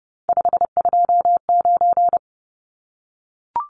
Morse code, heard as a keyed tone, about 30 wpm, 700 Hz. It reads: 529
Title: Numbers - 3-Digit 30wpm